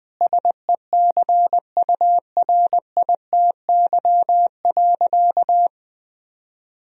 Morse code, heard as a keyed tone, about 20 wpm, 700 Hz. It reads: SECURITY.
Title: Sentences from Top 1000 Words - Review Part 4 - 20wpm